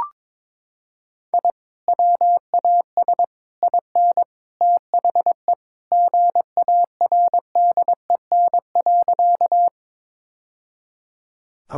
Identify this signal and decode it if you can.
Morse code, heard as a keyed tone, about 22 wpm, 700 Hz. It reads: I WAS IN THE GARDEN.
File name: practice / 22wpm / Sentences with 3-Word Phrases - 22wpm